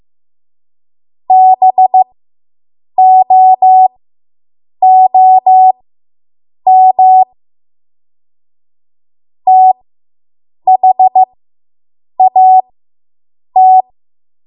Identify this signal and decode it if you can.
Morse code, heard as a keyed tone, about 15 wpm, spaced as for 7 wpm, 750 Hz. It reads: BOOM THAT